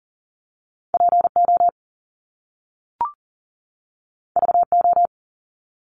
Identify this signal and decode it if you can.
Morse code, heard as a keyed tone, about 39 wpm, 700 Hz. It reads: PO VO
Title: Top Two Letter Combinations in Words 40wpm